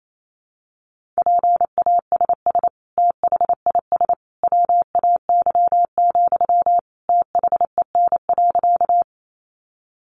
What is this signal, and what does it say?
Morse code, heard as a keyed tone, about 28 wpm, 700 Hz. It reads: PASS THIS WAY, THEN.
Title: Sentences from Top 400 Words 28wpm